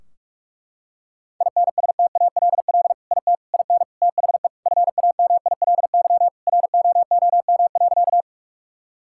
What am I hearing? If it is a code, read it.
Morse code, heard as a keyed tone, about 46 wpm, 700 Hz. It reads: INSTALL IT IN THE FAMILY ROOM.
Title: Sentences from Top 1600 Words 45wpm